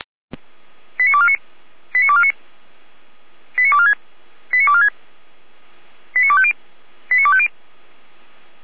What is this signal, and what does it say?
Five-tone selective call (ZVEI), heard as quick runs of five tones, tones of about 70 ms. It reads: 89249 89249 89246 89246 89250 89250